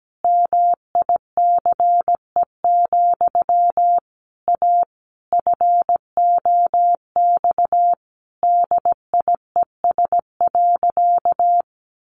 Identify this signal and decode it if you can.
Morse code, heard as a keyed tone, about 17 wpm, 700 Hz. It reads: MICE, A FOX DIES.
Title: A Sound Of Thunder 17wpm